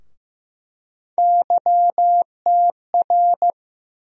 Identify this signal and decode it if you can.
Morse code, heard as a keyed tone, about 15 wpm, 700 Hz. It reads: YTR
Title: USA Call Sign Postfixes - 3 Letters 15wpm